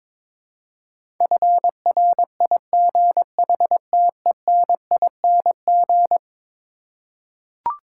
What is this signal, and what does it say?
Morse code, heard as a keyed tone, about 22 wpm, 700 Hz. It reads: FRIGHTENING
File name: practice / 22wpm / Eleven Letter Words - 22wpm